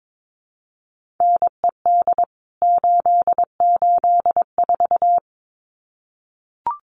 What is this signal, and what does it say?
Morse code, heard as a keyed tone, about 22 wpm, 700 Hz. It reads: NED 884
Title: CWT Exchange 22wpm